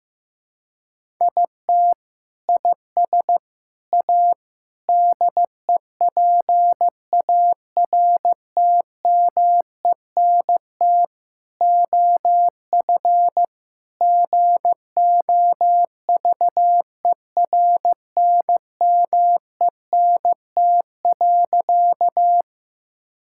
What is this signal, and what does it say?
Morse code, heard as a keyed tone, about 15 wpm, 700 Hz. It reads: IT IS A DEPARTMENT OF GOVERNMENT.
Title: Sentences from Top 1000 Words - Review Part 5 - 15wpm